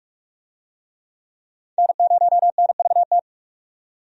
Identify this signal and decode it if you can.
Morse code, heard as a keyed tone, about 42 wpm, 700 Hz. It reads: N0NVT
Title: USA Call Signs - 5 Characters - 45wpm